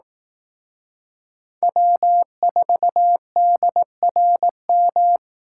Morse code, heard as a keyed tone, about 18 wpm, 700 Hz. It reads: W4DRM
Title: Speed Racing - USA 1x3 Call Signs 15wpm